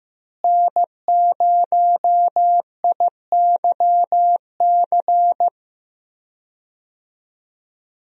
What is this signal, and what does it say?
Morse code, heard as a keyed tone, about 15 wpm, 700 Hz. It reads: N0IYC
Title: USA Call Signs - 5 Characters - Repeated 3x - 15wpm